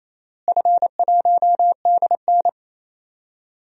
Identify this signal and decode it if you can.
Morse code, heard as a keyed tone, about 28 wpm, 700 Hz. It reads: F1DN
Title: DXCC-100 Call Signs - 4 Characters - Repeated 3x - 28wpm